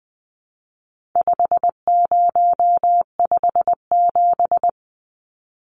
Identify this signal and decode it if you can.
Morse code, heard as a keyed tone, about 20 wpm, 700 Hz. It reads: 5057
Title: Numbers - 4-Digit 20wpm